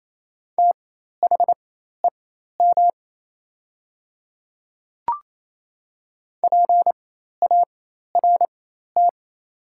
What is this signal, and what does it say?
Morse code, heard as a keyed tone, about 28 wpm, spaced as for 13 wpm, 690 Hz. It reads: THEM PART
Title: 4x Character Spacing - Top 200 Words - 28wpm